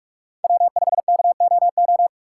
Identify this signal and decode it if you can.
Morse code, heard as a keyed tone, about 46 wpm, 700 Hz. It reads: W5KOO
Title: USA Call Signs - 5 Characters - 3x Character Spacing - 45wpm